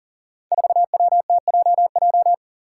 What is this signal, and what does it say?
Morse code, heard as a keyed tone, about 39 wpm, 700 Hz. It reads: 4WTJJ
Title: Five Challenging Letter-Number Combinations 40wpm